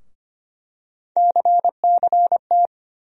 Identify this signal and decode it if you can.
Morse code, heard as a keyed tone, about 25 wpm, 700 Hz. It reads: CCT
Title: USA Call Sign Postfixes - 3 Letters 25wpm